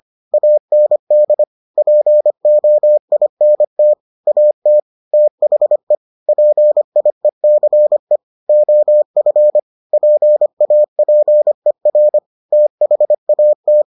Morse code, heard as a keyed tone, about 25 wpm, 600 Hz. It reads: AND POINT AT THE PIECE OF PAPER THAT